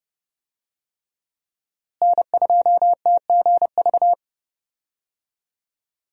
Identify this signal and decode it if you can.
Morse code, heard as a keyed tone, about 32 wpm, 700 Hz. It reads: N2TGV
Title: USA Call Signs - 5 Characters - Repeated 3x - 30wpm